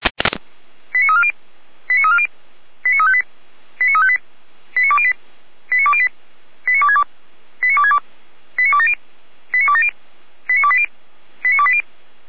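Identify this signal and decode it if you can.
Five-tone selective call (ZVEI), heard as quick runs of five tones, tones of about 70 ms. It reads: 89340 89340 89358 89358 89208 89208 89262 89262 89270 89270 89280 89280
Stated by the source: ZVEI five-tone selective call received on SDR, fire brigade Feuerwehr Orlamünde.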